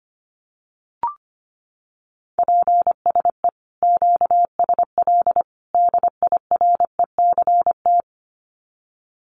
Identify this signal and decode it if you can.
Morse code, heard as a keyed tone, about 25 wpm, 700 Hz. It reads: PSE QSL DIRECT
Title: Common QSO Phrases 25wpm